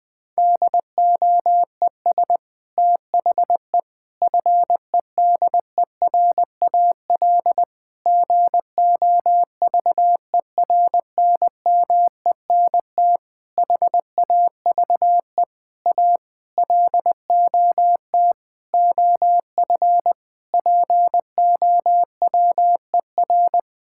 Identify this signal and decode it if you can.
Morse code, heard as a keyed tone, about 20 wpm, 700 Hz. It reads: DOES THE FEDERAL GOVERNMENT HAVE A LOT OF POWER
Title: Sentences from Top 1000 Words - Review Part 5 - 20wpm